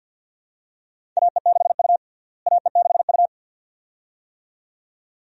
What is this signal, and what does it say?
Morse code, heard as a keyed tone, about 51 wpm, 700 Hz. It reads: AE6U AE6U
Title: USA Call Signs - 4 Characters - Repeated 2x - 50wpm